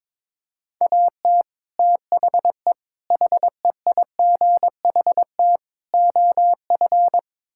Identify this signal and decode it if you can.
Morse code, heard as a keyed tone, about 22 wpm, 700 Hz. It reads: AT THE HEIGHT OF
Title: Sentences from Top 1600 Words 22wpm